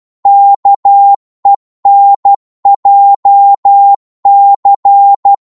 Morse code, heard as a keyed tone, about 12 wpm, 800 Hz. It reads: KENJC
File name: Morse / ltr-12wpm